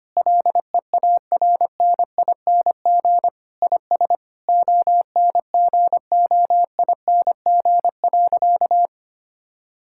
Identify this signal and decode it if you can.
Morse code, heard as a keyed tone, about 25 wpm, 700 Hz. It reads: LEARNING IS ONGOING.
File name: practice / 25wpm / Sentences with 3-Word Phrases - 25wpm